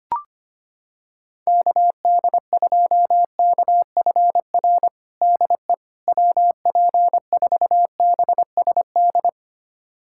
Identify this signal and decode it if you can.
Morse code, heard as a keyed tone, about 25 wpm, 700 Hz. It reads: KD2KFR DE WP4BSD
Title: USA Callsign DE Callsign 25wpm